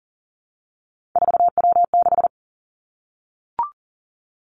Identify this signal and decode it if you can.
Morse code, heard as a keyed tone, about 40 wpm, 700 Hz. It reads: 4W6
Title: Three Challenging Letter-Number Combinations 40wpm